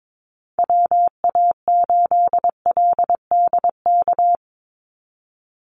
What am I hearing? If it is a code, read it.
Morse code, heard as a keyed tone, about 22 wpm, 700 Hz. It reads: WA8LDK
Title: USA Call Signs 22wpm